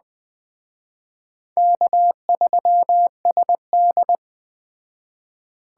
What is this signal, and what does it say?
Morse code, heard as a keyed tone, about 21 wpm, 700 Hz. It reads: K3SD